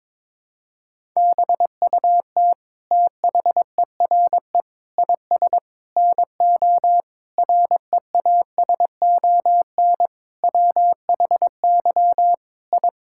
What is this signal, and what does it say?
Morse code, heard as a keyed tone, about 22 wpm, 700 Hz. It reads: BUT THERE IS NO REASON WHY I
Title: Sentences from Top 1000 Words - Review Part 6 - 22wpm